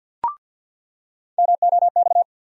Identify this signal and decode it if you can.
Morse code, heard as a keyed tone, about 54 wpm, 700 Hz. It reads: MOX
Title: USA Call Sign Postfixes - 3 Letters 50wpm